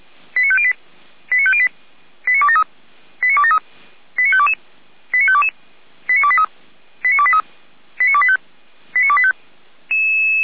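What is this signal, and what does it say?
Five-tone selective call (ZVEI), heard as quick runs of five tones, tones of about 70 ms. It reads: 89508 89508 89273 89273 89522 89522 89283 89283 89285 89285
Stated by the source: ZVEI five-tone selective call received on SDR, from the Feuerwehr Orlamuende audio archive.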